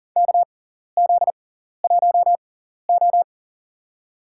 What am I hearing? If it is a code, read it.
Morse code, heard as a keyed tone, about 40 wpm, spaced as for 14 wpm, 700 Hz. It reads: KZ1O
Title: USA Call Signs - 4 Characters - 6x Character Spacing - 40wpm